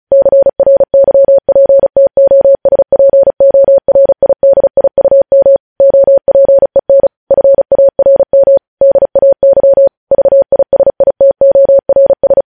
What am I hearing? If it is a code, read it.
Morse code, heard as a keyed tone, about 36 wpm, 550 Hz. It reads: CRYPTOSPORIDIUM OPEN FARM DAY VISITORS
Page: news headlines in morse code at 35 wpm